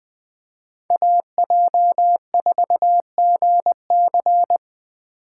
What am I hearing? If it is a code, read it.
Morse code, heard as a keyed tone, about 20 wpm, 700 Hz. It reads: AJ4GC
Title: Speed Racing - USA 2x2 Call Signs 17wpm